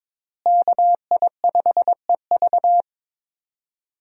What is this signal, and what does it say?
Morse code, heard as a keyed tone, about 22 wpm, 700 Hz. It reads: KI5EV
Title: USA Call Signs - 5 Characters - Repeated 3x - 22wpm